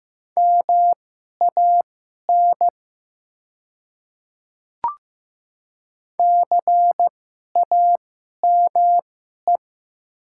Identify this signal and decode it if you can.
Morse code, heard as a keyed tone, about 15 wpm, spaced as for 11 wpm, 700 Hz. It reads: MAN CAME
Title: 2x Character Spacing - Top 200 Words - 15wpm